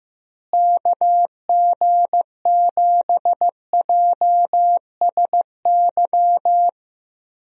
Morse code, heard as a keyed tone, about 15 wpm, 700 Hz. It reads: KG7JSY
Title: USA Call Signs - 6 Characters - Repeated 2x - 15wpm